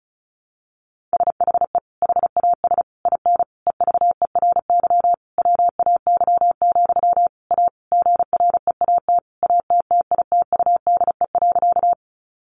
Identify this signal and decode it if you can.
Morse code, heard as a keyed tone, about 34 wpm, 700 Hz. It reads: SHE HAS IN EVERY WAY, A GREAT ATTITUDE.
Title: Sentences from Top 1300 Words 35wpm